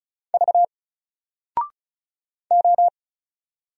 Morse code, heard as a keyed tone, about 36 wpm, 700 Hz. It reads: V O
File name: Single-Letter-Number-35wpm